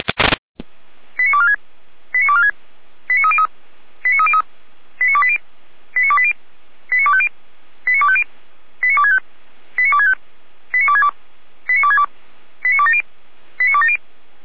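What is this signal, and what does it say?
Five-tone selective call (ZVEI), heard as quick runs of five tones, tones of about 70 ms. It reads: 89246 89246 89303 89303 89280 89280 89250 89250 89265 89265 89262 89262 89270 89270
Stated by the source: ZVEI five-tone selective call received by SDR, recorded by Feuerwehr Orlamünde (fire brigade).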